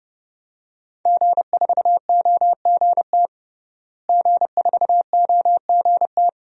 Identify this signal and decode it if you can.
Morse code, heard as a keyed tone, about 32 wpm, 700 Hz. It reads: G4OGT G4OGT
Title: DXCC-100 Call Signs - 5 Characters - Repeated 2x - 30wpm